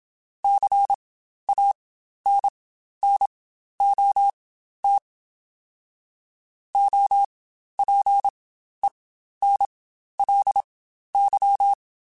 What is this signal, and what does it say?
Morse code, heard as a keyed tone, about 27 wpm, spaced as for 12 wpm, 800 Hz. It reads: CANNOT OPENLY
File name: Quotes Spaced / Quotes Spaced-mp3s / Advice sp @25